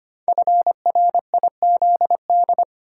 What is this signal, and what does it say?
Morse code, heard as a keyed tone, about 25 wpm, 700 Hz. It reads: FRIZD
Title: Code Groups - 5 Letter:Numbers - 25wpm